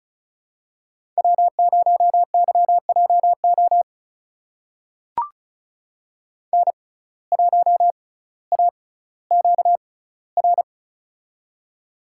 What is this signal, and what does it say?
Morse code, heard as a keyed tone, about 36 wpm, spaced as for 18 wpm, 700 Hz. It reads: W0YJO N1AQR
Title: USA Call Signs - 5 Characters - 6x Character Spacing - 35wpm